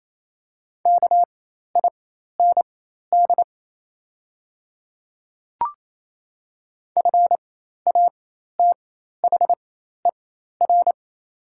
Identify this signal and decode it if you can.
Morse code, heard as a keyed tone, about 27 wpm, spaced as for 13 wpm, 690 Hz. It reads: KIND FATHER
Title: 4x Character Spacing - Top 200 Words - 28wpm